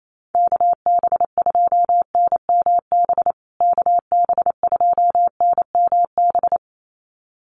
Morse code, heard as a keyed tone, about 28 wpm, 700 Hz. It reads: KB2NMB KB2NMB
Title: USA Call Signs Repeated 2x 28wpm